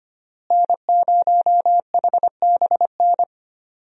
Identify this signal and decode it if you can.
Morse code, heard as a keyed tone, about 25 wpm, 700 Hz. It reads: N0HBN
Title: USA Call Signs - 5 Characters - Repeated 3x - 25wpm